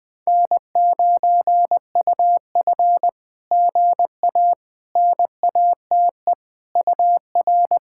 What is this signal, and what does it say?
Morse code, heard as a keyed tone, about 20 wpm, 660 Hz. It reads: N9UF GA NATE UR
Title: SOTA QSO Practice 20wpm